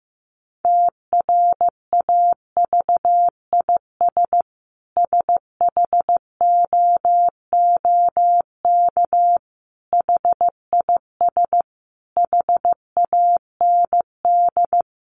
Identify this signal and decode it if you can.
Morse code, heard as a keyed tone, about 15 wpm, 700 Hz. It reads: TRAVIS SHOOK HIS HAND
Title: A Sound Of Thunder 15wpm